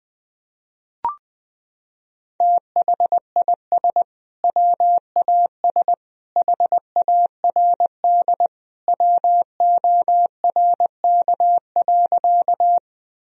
Morse code, heard as a keyed tone, about 20 wpm, 700 Hz. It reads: THIS WAS HARD WORK.
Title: Sentences from Top 1000 Words - Review Part 1 - 20wpm